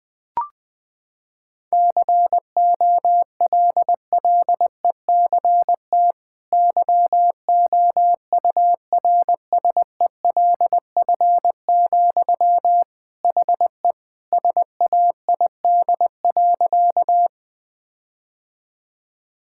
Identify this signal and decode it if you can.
Morse code, heard as a keyed tone, about 20 wpm, 700 Hz. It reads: COLLECT YOURSELF, HE SAID.
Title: Sentences from Top 1600 Words 20wpm